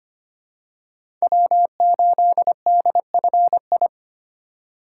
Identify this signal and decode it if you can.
Morse code, heard as a keyed tone, about 25 wpm, 700 Hz. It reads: W8DFI
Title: USA Call Signs - 5 Characters - 4x Character Spacing - 25wpm